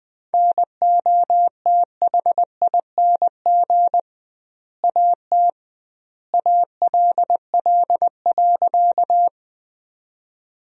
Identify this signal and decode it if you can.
Morse code, heard as a keyed tone, about 21 wpm, 700 Hz. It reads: NOTHING AT ALL.
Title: 2x Word-Spacing - Sentences from Top 500 Words - 20wpm